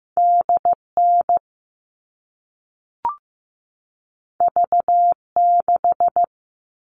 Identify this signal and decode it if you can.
Morse code, heard as a keyed tone, about 15 wpm, 700 Hz. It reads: DN V6
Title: All Cut Numbers - 2-Digit 15wpm